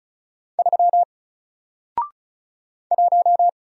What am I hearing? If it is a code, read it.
Morse code, heard as a keyed tone, about 34 wpm, 700 Hz. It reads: 3 1
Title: Single Letter-Number Alt 35wpm